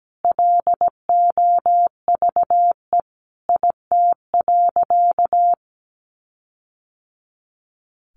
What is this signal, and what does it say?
Morse code, heard as a keyed tone, about 17 wpm, 700 Hz. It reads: LOVE IT.
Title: Sentences from Top 1500 Words 17wpm